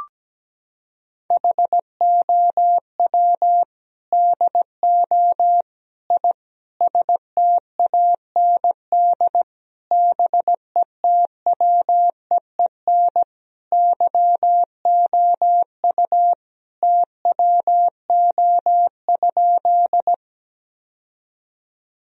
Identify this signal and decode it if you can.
Morse code, heard as a keyed tone, about 17 wpm, 700 Hz. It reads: HOW DO I STAND BETWEEN YOU TWO?
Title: Sentences from Top 1000 Words - Review Part 1 - 17wpm